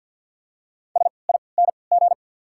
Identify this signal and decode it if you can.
Morse code, heard as a keyed tone, about 52 wpm, spaced as for 29 wpm, 700 Hz. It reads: SING
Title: 3x Character Spacing - Top 500 Words - 50wpm